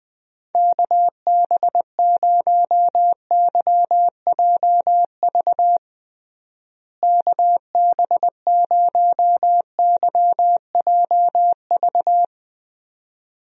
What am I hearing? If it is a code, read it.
Morse code, heard as a keyed tone, about 21 wpm, 700 Hz. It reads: KB0YJV KB0YJV
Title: USA Call Signs - 6 Characters - Repeated 2x - 20wpm